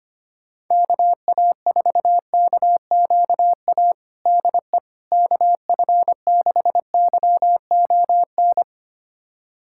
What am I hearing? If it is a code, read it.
Morse code, heard as a keyed tone, about 25 wpm, 700 Hz. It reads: KA4KQA DE KF6YON
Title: USA Callsign DE Callsign 25wpm